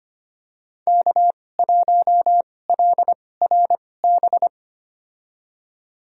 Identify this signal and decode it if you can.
Morse code, heard as a keyed tone, about 25 wpm, spaced as for 18 wpm, 700 Hz. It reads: K1LRB
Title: USA Call Signs - 5 Characters - 2x Character Spacing - 25wpm